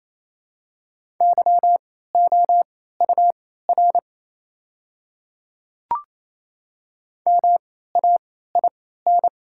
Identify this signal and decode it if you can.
Morse code, heard as a keyed tone, about 28 wpm, spaced as for 16 wpm, 700 Hz. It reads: YOUR MAIN
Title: 3x Character Spacing - Top 300 Words - 28wpm